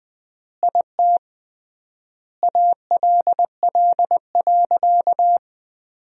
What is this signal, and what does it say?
Morse code, heard as a keyed tone, about 20 wpm, 700 Hz. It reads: IT ALL.
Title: 3x Word-Spacing - Sentences from Top 300 Words - 20wpm